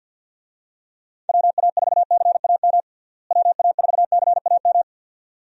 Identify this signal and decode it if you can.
Morse code, heard as a keyed tone, about 51 wpm, 700 Hz. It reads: WA4CAM WA4CAM
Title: USA Call Signs - 6 Characters - Repeated 2x - 50wpm